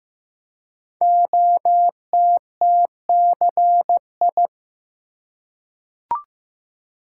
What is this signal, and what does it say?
Morse code, heard as a keyed tone, about 15 wpm, 700 Hz. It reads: OTTCI
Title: Code Groups - 5 Letters - 15wpm